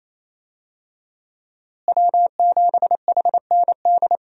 Morse code, heard as a keyed tone, about 28 wpm, 700 Hz. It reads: W7HND
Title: USA Call Signs - 5 Characters - Repeated 2x - 28wpm